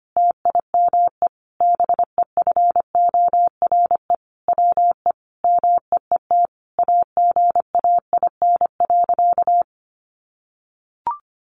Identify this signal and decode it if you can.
Morse code, heard as a keyed tone, about 25 wpm, 700 Hz. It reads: TIME BEFORE WE MEET AGAIN.
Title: Sentences from Top 1100 Words 25wpm